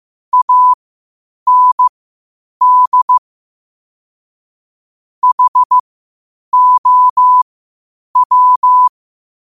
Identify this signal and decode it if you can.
Morse code, heard as a keyed tone, about 15 wpm, spaced as for 8 wpm, 1000 Hz. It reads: AND HOW
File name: qotdfiles / 8wpm